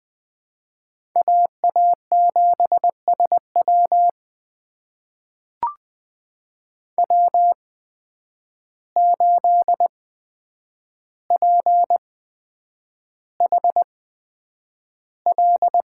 Morse code, heard as a keyed tone, about 20 wpm, spaced as for 9 wpm, 700 Hz. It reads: AA7SW W8PHL
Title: USA Call Signs - 5 Characters - 8x Character Spacing - 20wpm